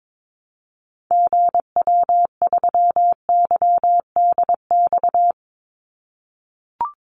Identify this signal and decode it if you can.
Morse code, heard as a keyed tone, about 22 wpm, 700 Hz. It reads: GW3YDX
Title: Call Signs - 22wpm